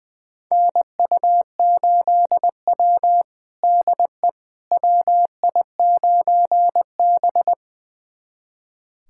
Morse code, heard as a keyed tone, about 21 wpm, 700 Hz. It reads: NU8W DE WI9B
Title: USA Callsign DE Callsign - 4 Characters - 20wpm